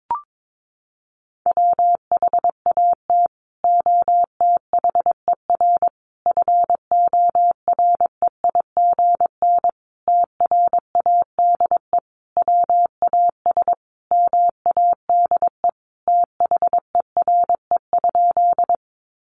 Morse code, heard as a keyed tone, about 22 wpm, 700 Hz. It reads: WHAT OTHER FOREIGN TRADE WAS MADE THERE?
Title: Sentences from Top 1400 Words 22wpm